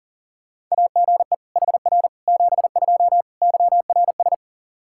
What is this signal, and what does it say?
Morse code, heard as a keyed tone, about 37 wpm, 700 Hz. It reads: AGE HR 72 YRS
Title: K1USN SST QSO Practice 40wpm